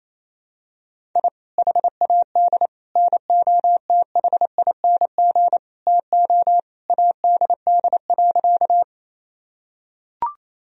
Morse code, heard as a keyed tone, about 28 wpm, 700 Hz. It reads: I HAD NOTHING TO ADD.
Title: Sentences from Top 1000 Words - Review Part 1 - 28wpm